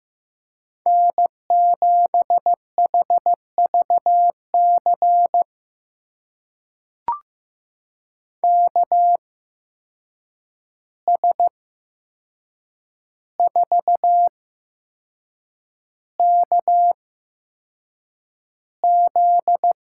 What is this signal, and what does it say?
Morse code, heard as a keyed tone, about 15 wpm, spaced as for 6 wpm, 700 Hz. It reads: N7HVC KS4KZ
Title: USA Call Signs - 5 Characters - 8x Character Spacing - 15wpm